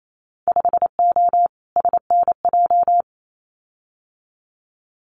Morse code, heard as a keyed tone, about 28 wpm, 700 Hz. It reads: 5O SNJ
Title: Winter Field Day Exchanges Encore 28wpm